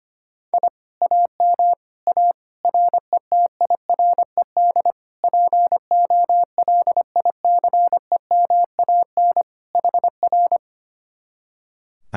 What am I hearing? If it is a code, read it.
Morse code, heard as a keyed tone, about 24 wpm, 700 Hz. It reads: I AM A RETIRED POLICEMAN HR
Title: Speed-Racing Common QSO Phrases - 25wpm